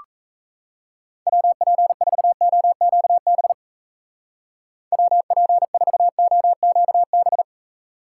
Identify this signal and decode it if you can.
Morse code, heard as a keyed tone, about 34 wpm, 700 Hz. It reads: WP4OQB WP4OQB
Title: Speed Racing - USA 2x3 Call Signs 28wpm